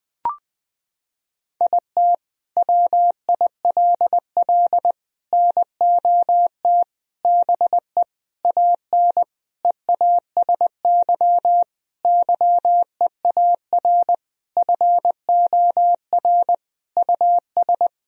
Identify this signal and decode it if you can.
Morse code, heard as a keyed tone, about 20 wpm, 700 Hz. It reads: IT WILL NOT BE AN EASY YEAR FOR US
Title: Sentences from Top 1000 Words - Review Part 2 - 20wpm